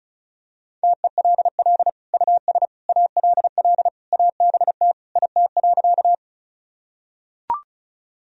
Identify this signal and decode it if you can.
Morse code, heard as a keyed tone, about 36 wpm, 700 Hz. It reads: TELL US ALL ABT IT.
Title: Sentences from Top 200 Words and Abbreviations 35wpm